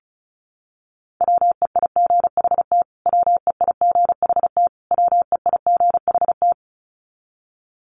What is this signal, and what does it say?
Morse code, heard as a keyed tone, about 34 wpm, 700 Hz. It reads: WEIGHT WEIGHT WEIGHT
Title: Top 500 Words Repeated 35wpm